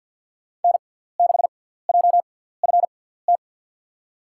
Morse code, heard as a keyed tone, about 52 wpm, spaced as for 17 wpm, 700 Hz. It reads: N6JFT
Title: USA Call Signs - 5 Characters - 6x Character Spacing - 50wpm